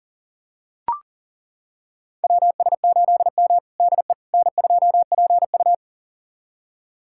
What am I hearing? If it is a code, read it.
Morse code, heard as a keyed tone, about 40 wpm, 700 Hz. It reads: WS8M DE N2PU
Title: USA Callsign DE Callsign - 4 Characters - 40wpm